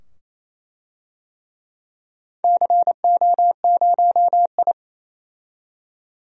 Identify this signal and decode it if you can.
Morse code, heard as a keyed tone, about 28 wpm, 700 Hz. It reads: CO0I
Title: DXCC-100 Call Signs - 4 Characters - 28wpm